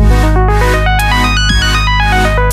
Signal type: none